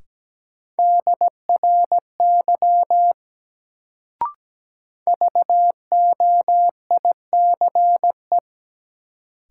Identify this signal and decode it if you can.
Morse code, heard as a keyed tone, about 17 wpm, 700 Hz. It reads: DRY VOICE
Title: Sentences from Top 500 Words - New Word Drill - 17wpm